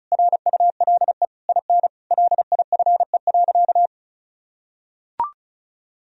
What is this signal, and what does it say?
Morse code, heard as a keyed tone, about 34 wpm, 700 Hz. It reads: RULE IN LIFE.
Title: Sentences from Top 1000 Words - Review Part 2 - 35wpm